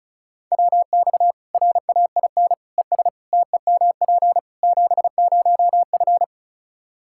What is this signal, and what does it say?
Morse code, heard as a keyed tone, about 36 wpm, 700 Hz. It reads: WX RAIN ES TEMP 70F
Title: Practice QSO - Exchange 2 - Info - 35wpm